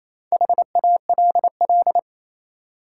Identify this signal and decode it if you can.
Morse code, heard as a keyed tone, about 28 wpm, 700 Hz. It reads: HALL
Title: Four Letter Words - 28wpm